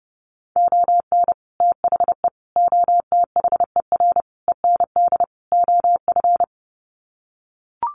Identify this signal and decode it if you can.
Morse code, heard as a keyed tone, about 30 wpm, 700 Hz. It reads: ON THE OTHER END OF
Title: Sets of 5 Words Encore 30wpm